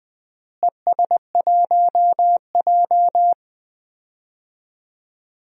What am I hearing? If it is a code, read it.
Morse code, heard as a keyed tone, about 19 wpm, 690 Hz. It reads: ES1J
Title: DXCC-100 Call Signs - 4 Characters - Repeated 3x - 20wpm